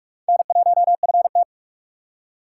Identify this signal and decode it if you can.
Morse code, heard as a keyed tone, about 43 wpm, 700 Hz. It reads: N1FT